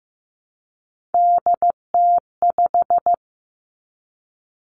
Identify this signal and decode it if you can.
Morse code, heard as a keyed tone, about 15 wpm, 700 Hz. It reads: DT5
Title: Three Challenging Letter-Number Combinations 15wpm